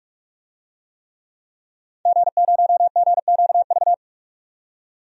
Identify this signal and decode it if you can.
Morse code, heard as a keyed tone, about 46 wpm, 700 Hz. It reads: G0GQV
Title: DXCC-100 Call Signs - 5 Characters - 45wpm